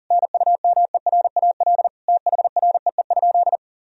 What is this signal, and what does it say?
Morse code, heard as a keyed tone, about 40 wpm, 700 Hz. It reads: NUMERAL THREE?
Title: Sentences from Top 1000 Words - Review Part 2 - 40wpm